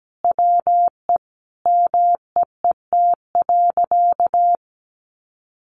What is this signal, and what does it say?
Morse code, heard as a keyed tone, about 17 wpm, 700 Hz. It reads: WE MEET.
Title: Sentences from Top 1500 Words 17wpm